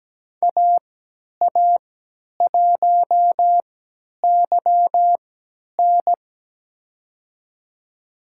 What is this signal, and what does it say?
Morse code, heard as a keyed tone, about 17 wpm, spaced as for 10 wpm, 700 Hz. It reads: AA1YN
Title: USA Call Signs - 5 Characters - 3x Character Spacing - 17wpm